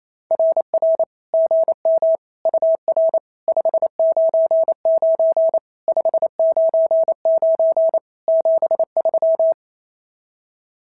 Morse code, heard as a keyed tone, about 28 wpm, 650 Hz. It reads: RR GM UR 599 599 73
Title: SOTA QSO Practice 28wpm